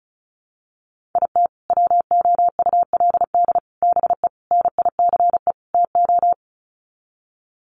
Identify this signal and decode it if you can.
Morse code, heard as a keyed tone, about 34 wpm, 700 Hz. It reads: IT WOULD BE NICE TO
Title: Sets of 5 Words Encore2 35wpm